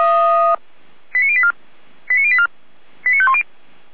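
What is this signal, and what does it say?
Five-tone selective call (ZVEI), heard as quick runs of five tones, tones of about 70 ms. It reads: 89084 89084 89510
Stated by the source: ZVEI five-tone call received by SDR.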